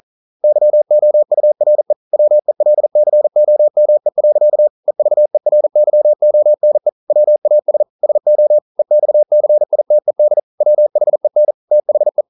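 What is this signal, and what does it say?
Morse code, heard as a keyed tone, about 43 wpm, 600 Hz. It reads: YOURE WELCOME. EVERYONE WAS SO EXCITED WHEN THE
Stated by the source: Morse code at 40 wpm, practice text, CW med Gnister.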